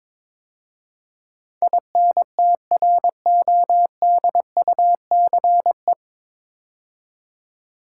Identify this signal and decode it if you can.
Morse code, heard as a keyed tone, about 22 wpm, 700 Hz. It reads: INTRODUCE
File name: 9 Letter Words - First 3 Letters Given - 22wpm